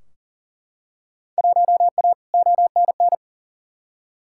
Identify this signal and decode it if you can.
Morse code, heard as a keyed tone, about 40 wpm, 700 Hz. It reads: 1A ONN
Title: Field Day Exchanges Encore - 40wpm